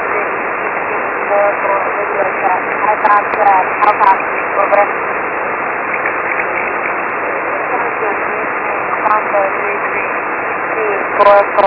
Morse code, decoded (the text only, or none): none